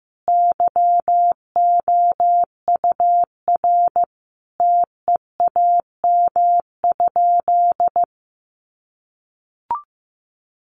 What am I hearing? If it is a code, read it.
Morse code, heard as a keyed tone, about 15 wpm, 700 Hz. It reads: YOUR TEAM?
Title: Sentences from Top 1500 Words 15wpm